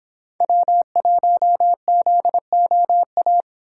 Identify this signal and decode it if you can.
Morse code, heard as a keyed tone, about 26 wpm, 700 Hz. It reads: W1ZOA